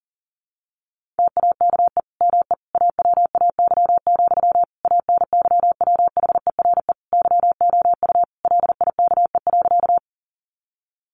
Morse code, heard as a keyed tone, about 40 wpm, 700 Hz. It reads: TAKE ME AWAY, ANYWHERE YOU LIKE.
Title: Sentences from Top 1400 Words 40wpm